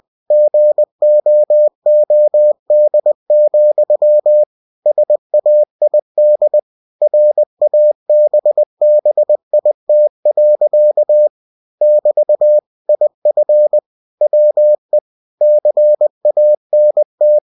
Morse code, heard as a keyed tone, about 20 wpm, 600 Hz. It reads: GOOD, SAID RABBIT. <BT> IF WE CANT
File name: cw-books / winnie-the-pooh / winnie-20wpm-page-0018